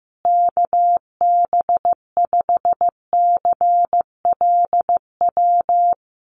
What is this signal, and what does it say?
Morse code, heard as a keyed tone, about 15 wpm, 700 Hz. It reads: KB5CLW
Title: USA Call Signs Repeated 2x 15wpm